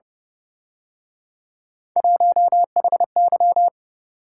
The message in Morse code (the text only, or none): 1HY